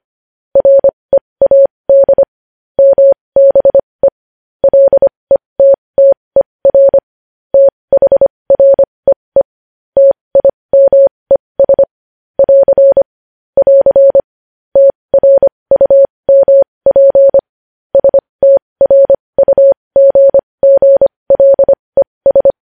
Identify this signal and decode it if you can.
Morse code, heard as a keyed tone, about 25 wpm, spaced as for 20 wpm, 550 Hz. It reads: READ MBE LETTER THREE TIMES <AR> <AR> TRUMP STRUGGLES
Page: News Headlines in Morse Code at 20 WPM